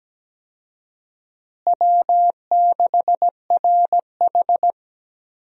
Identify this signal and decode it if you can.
Morse code, heard as a keyed tone, about 17 wpm, 700 Hz. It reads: W6RH
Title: USA Call Signs - 4 Characters - Repeated 3x - 17wpm